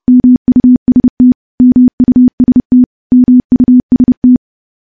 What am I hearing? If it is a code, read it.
Morse code, heard as a keyed tone, about 32 wpm, 250 Hz. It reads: MUST MUST MUST
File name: morse-training-30WPM